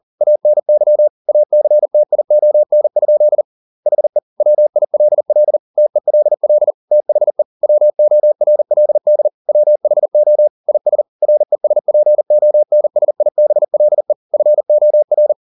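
Morse code, heard as a keyed tone, about 40 wpm, 600 Hz. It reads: ANY ACTION? HE WILL TELL THE WORLD WHO IS RESPONSIBLE FOR
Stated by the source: Morse code at 40 wpm, practice text, CW med Gnister.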